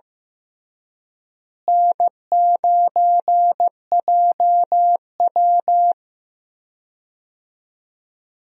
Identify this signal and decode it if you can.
Morse code, heard as a keyed tone, about 15 wpm, 700 Hz. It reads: N9JW
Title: USA Call Signs - 4 Characters - Repeated 3x - 15wpm